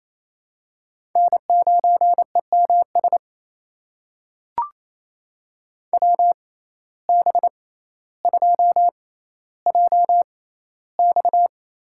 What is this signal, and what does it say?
Morse code, heard as a keyed tone, about 28 wpm, spaced as for 14 wpm, 700 Hz. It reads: N9EMS WB2JX
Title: USA Call Signs - 6x Character Spacing - 28wpm